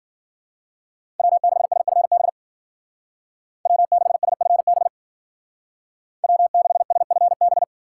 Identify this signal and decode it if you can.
Morse code, heard as a keyed tone, about 50 wpm, 700 Hz. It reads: W6SFB W6SFB W6SFB
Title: Speed Racing - USA 1x3 Call Signs 40wpm